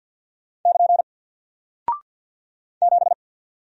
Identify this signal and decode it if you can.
Morse code, heard as a keyed tone, about 48 wpm, 700 Hz. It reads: <KN> 7
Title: Single Character (Letter:Number:Punctuation:Prosign) Alt 50wpm